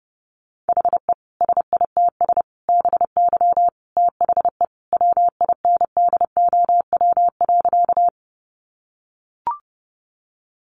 Morse code, heard as a keyed tone, about 28 wpm, 700 Hz. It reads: HE SITS BY THE WINDOW.
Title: Additional Easy Phrases 30wpm